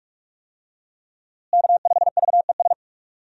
Morse code, heard as a keyed tone, about 46 wpm, 700 Hz. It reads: K5VES
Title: USA Call Signs - 5 Characters - Repeated 3x - 45wpm